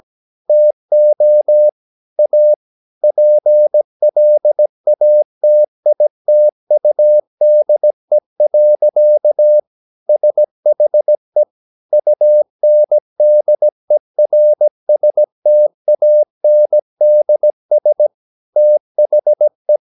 Morse code, heard as a keyed tone, about 17 wpm, 600 Hz. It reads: TO A PLATITUDE. SHE UNDERSTANDS THE
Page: Romanen Never af Ken Follet på engelsk.